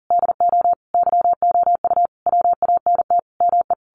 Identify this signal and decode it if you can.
Morse code, heard as a keyed tone, about 40 wpm, 700 Hz. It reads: DO YOU WANT ME
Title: A Sound Of Thunder 40wpm